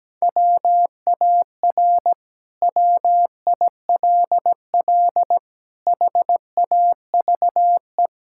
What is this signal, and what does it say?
Morse code, heard as a keyed tone, about 17 wpm, 700 Hz. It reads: WAR WILL HAVE
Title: Sentences from Top 1000 Words - Review Part 6 - 17wpm